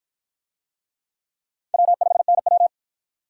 Morse code, heard as a keyed tone, about 46 wpm, 700 Hz. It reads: W5NW